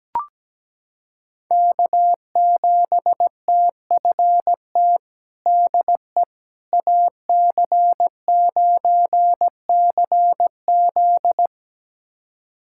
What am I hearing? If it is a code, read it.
Morse code, heard as a keyed tone, about 17 wpm, 700 Hz. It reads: K7TFT DE AC9CZ
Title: USA Callsign DE Callsign - 5 Characters - 17wpm